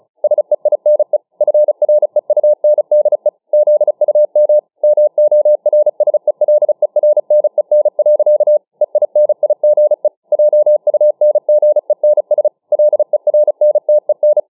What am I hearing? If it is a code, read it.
Morse code, heard as a keyed tone, about 36 wpm, 600 Hz. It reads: SEINE FREUNDE ZUM MORSELERNEN. EINIGE JUNGENS LERNTEN